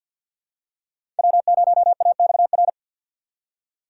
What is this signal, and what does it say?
Morse code, heard as a keyed tone, about 53 wpm, 700 Hz. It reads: W0AXR
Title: USA Call Signs - 4x Character Spacing - 50wpm